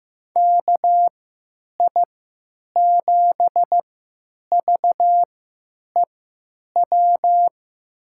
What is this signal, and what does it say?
Morse code, heard as a keyed tone, about 15 wpm, spaced as for 9 wpm, 700 Hz. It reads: KI7VEW
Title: USA Call Signs - 3x Character Spacing - 15wpm